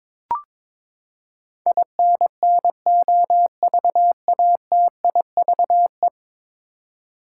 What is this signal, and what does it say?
Morse code, heard as a keyed tone, about 22 wpm, 700 Hz. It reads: INNOVATIVE
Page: morse-code-ninja-practice-22wpm